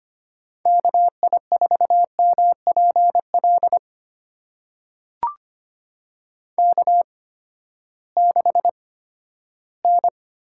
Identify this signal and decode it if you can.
Morse code, heard as a keyed tone, about 25 wpm, spaced as for 14 wpm, 700 Hz. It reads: KI4MPL K6N
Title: USA Call Signs - 8x Character Spacing - 25wpm